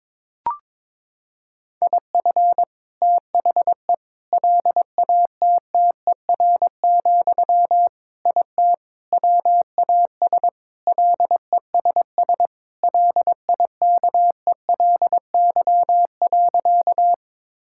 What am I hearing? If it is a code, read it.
Morse code, heard as a keyed tone, about 22 wpm, 700 Hz. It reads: IF THE LATTER, IT WAS LESS LIKELY.
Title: Sentences from Top 1600 Words 22wpm